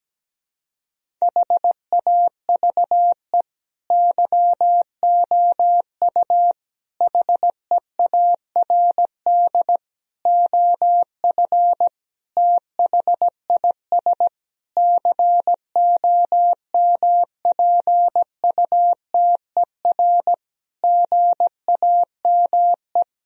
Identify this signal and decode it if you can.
Morse code, heard as a keyed tone, about 17 wpm, 700 Hz. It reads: HAVE YOU HEARD OF THIS COMPUTER GAME
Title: Sentences from Top 1000 Words - Review Part 3 - 17wpm